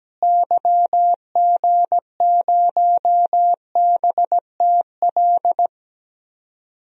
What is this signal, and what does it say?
Morse code, heard as a keyed tone, about 17 wpm, 700 Hz. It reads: YG0BTL
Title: DXCC-100 Call Signs - 6 Characters - Repeated 2x - 17wpm